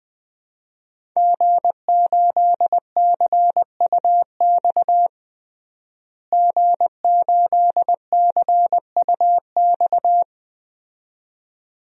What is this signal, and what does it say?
Morse code, heard as a keyed tone, about 20 wpm, 700 Hz. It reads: G8CUX G8CUX
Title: DXCC-100 Call Signs - 5 Characters - Repeated 2x - 20wpm